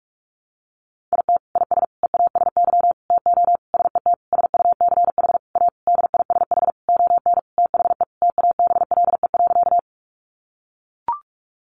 Morse code, heard as a keyed tone, about 44 wpm, 700 Hz. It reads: IT IS EASY TO SET SUCH A DISH ON THE TABLE.
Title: Sentences from Top 1300 Words 45wpm